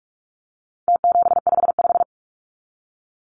Morse code, heard as a keyed tone, about 44 wpm, 700 Hz. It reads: T755
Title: Numbers - Contest Style - 4-Digit 45wpm